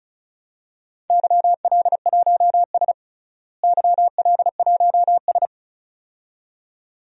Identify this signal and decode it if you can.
Morse code, heard as a keyed tone, about 34 wpm, 700 Hz. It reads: YL1S YL1S
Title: DXCC-100 Call Signs - 4 Characters - Repeated 2x - 35wpm